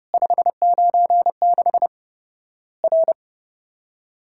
Morse code, heard as a keyed tone, about 31 wpm, 700 Hz. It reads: 596 R
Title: Worked All Europe DX Contest - QTC Exchange - 30wpm